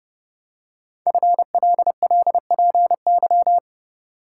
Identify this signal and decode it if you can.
Morse code, heard as a keyed tone, about 30 wpm, 700 Hz. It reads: FLLPY
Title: Code Groups - 5 Letters - 30wpm